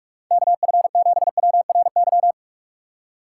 Morse code, heard as a keyed tone, about 46 wpm, 700 Hz. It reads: KF7WRY
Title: USA Call Signs - 6 Characters - 3x Character Spacing - 45wpm